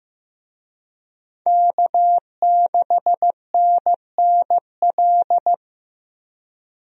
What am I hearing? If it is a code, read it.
Morse code, heard as a keyed tone, about 15 wpm, 700 Hz. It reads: K6NNL
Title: USA Call Signs - 5 Characters - Repeated 3x - 15wpm